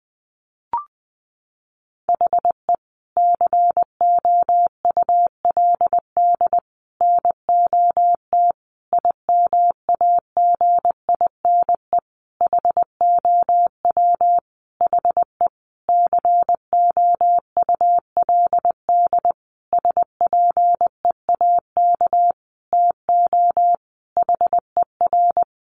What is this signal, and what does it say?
Morse code, heard as a keyed tone, about 20 wpm, 700 Hz. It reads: HE COULD NOT IMAGINE HOW HE COULD SPEAK TO HER
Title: Sentences from Top 1400 Words 20wpm